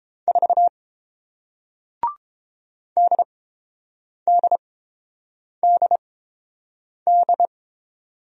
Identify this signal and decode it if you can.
Morse code, heard as a keyed tone, about 32 wpm, spaced as for 7 wpm, 690 Hz. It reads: 4 DDDD